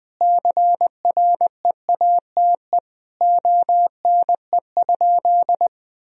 Morse code, heard as a keyed tone, about 21 wpm, 700 Hz. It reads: CREATE ONE?
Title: Sentences from Top 1000 Words - Review Part 3 - 20wpm